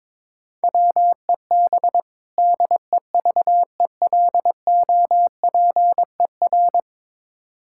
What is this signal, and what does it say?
Morse code, heard as a keyed tone, about 22 wpm, 700 Hz. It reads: WEB DEVELOPER
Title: K1USN SST QSO Practice 22wpm